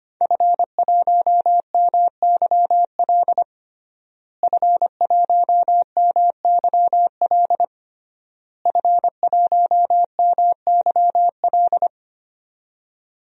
Morse code, heard as a keyed tone, about 25 wpm, 700 Hz. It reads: F1MYL F1MYL F1MYL
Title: DXCC-100 Call Signs - 5 Characters - Repeated 3x - 25wpm